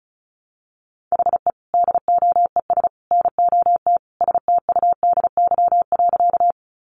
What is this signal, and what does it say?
Morse code, heard as a keyed tone, about 34 wpm, 700 Hz. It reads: HE DOES NOT STUDY.
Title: Additional Easy Phrases 35wpm